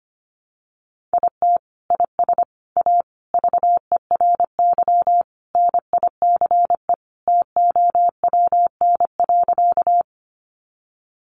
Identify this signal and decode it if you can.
Morse code, heard as a keyed tone, about 25 wpm, 700 Hz. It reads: IT IS A VERY NICE TOWN.
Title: Sentences from Top 1100 Words 25wpm